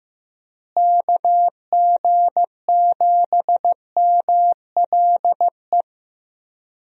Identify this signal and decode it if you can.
Morse code, heard as a keyed tone, about 15 wpm, 700 Hz. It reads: KG7MLE
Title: USA Call Signs - 6 Characters - Repeated 2x - 15wpm